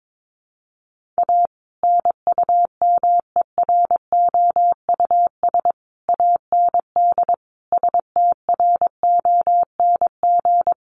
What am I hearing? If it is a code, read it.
Morse code, heard as a keyed tone, about 22 wpm, 700 Hz. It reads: A NUMEROUS AND STRONG
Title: Sentences from Top 1400 Words 22wpm